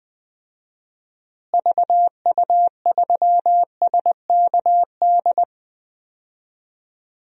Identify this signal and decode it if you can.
Morse code, heard as a keyed tone, about 21 wpm, 700 Hz. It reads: VU3SKD
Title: DXCC-100 Call Signs - 6 Characters - Repeated 2x - 20wpm